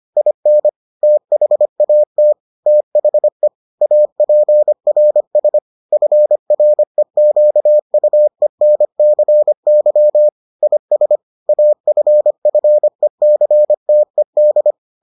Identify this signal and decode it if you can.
Morse code, heard as a keyed tone, about 25 wpm, 600 Hz. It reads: IN THAT THE APRS FREQUENCY IS AFFECTED